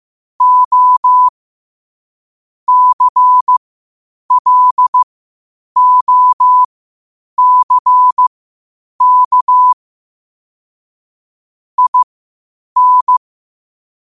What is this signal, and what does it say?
Morse code, heard as a keyed tone, about 15 wpm, spaced as for 8 wpm, 1000 Hz. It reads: O CLOCK IN
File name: qotdfiles / 8wpm